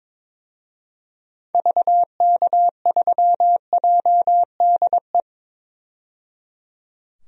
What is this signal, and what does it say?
Morse code, heard as a keyed tone, about 22 wpm, 700 Hz. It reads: VK3JDE
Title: DXCC-100 Call Signs - 6 Characters - 22wpm